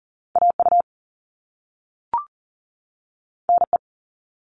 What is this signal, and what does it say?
Morse code, heard as a keyed tone, about 41 wpm, 700 Hz. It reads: AU NE
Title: All Cut Numbers - 2-Digit 40wpm